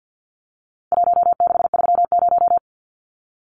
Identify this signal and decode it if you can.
Morse code, heard as a keyed tone, about 50 wpm, 700 Hz. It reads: 1630
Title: Numbers - Contest Style - 4-Digit 50wpm